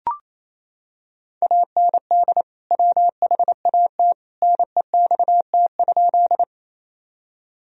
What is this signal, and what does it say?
Morse code, heard as a keyed tone, about 28 wpm, 700 Hz. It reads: AND WHAT NEXT?
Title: Sentences from Top 1000 Words - Review Part 1 - 28wpm